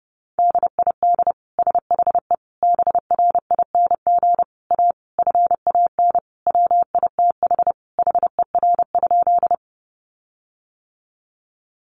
Morse code, heard as a keyed tone, about 28 wpm, 700 Hz. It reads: DID SHE BRING A FAN WITH HER?
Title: Sentences from Top 1300 Words 30wpm